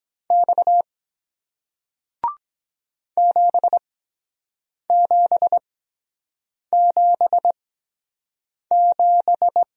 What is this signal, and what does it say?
Morse code, heard as a keyed tone, about 23 wpm, spaced as for 7 wpm, 700 Hz. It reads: X 7777